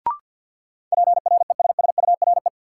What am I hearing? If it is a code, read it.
Morse code, heard as a keyed tone, about 53 wpm, 700 Hz. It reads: PRESSURE
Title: Sentences from Top 1400 Words - New Word Drill Alt - 50wpm